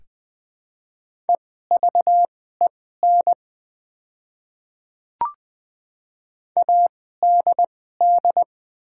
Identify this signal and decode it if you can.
Morse code, heard as a keyed tone, about 20 wpm, spaced as for 14 wpm, 700 Hz. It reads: EVEN ADD
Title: 2x Character Spacing - Top 300 Words - 20wpm